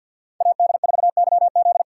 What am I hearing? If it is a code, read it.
Morse code, heard as a keyed tone, about 52 wpm, 700 Hz. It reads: AD4YZ
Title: USA Call Signs - 5 Characters - 2x Character Spacing - 50wpm